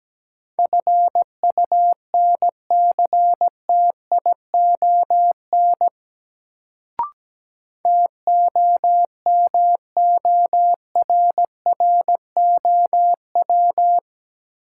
Morse code, heard as a keyed tone, about 17 wpm, 700 Hz. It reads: FUNCTION TOMORROW
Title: Sentences from Top 1100 Words - New Word Drill Alt - 17wpm